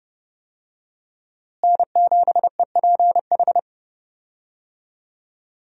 Morse code, heard as a keyed tone, about 32 wpm, 700 Hz. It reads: N7EPH
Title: USA Call Signs - 5 Characters - Repeated 2x - 30wpm